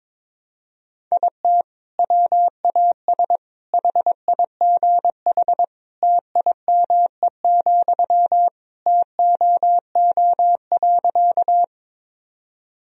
Morse code, heard as a keyed tone, about 22 wpm, 700 Hz. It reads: IT WAS HIGH TIME, TOO.
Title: Sentences from Top 1000 Words - Review Part 1 - 22wpm